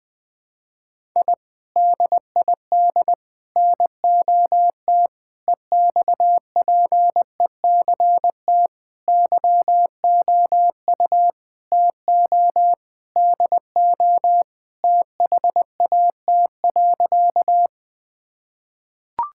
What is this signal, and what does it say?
Morse code, heard as a keyed tone, about 21 wpm, 700 Hz. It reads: I DID NOT EXPECT YOU TO DO THAT.
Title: Sentences with 3-Word Phrases - 20wpm